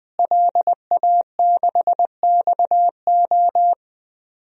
Morse code, heard as a keyed tone, about 20 wpm, 700 Hz. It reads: LA6XO
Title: DXCC-100 Call Signs - 5 Characters - Repeated 3x - 20wpm